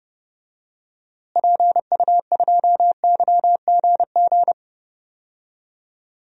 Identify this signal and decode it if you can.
Morse code, heard as a keyed tone, about 30 wpm, 700 Hz. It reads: PU2YGG
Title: DXCC-100 Call Signs - 6 Characters - Repeated 3x - 30wpm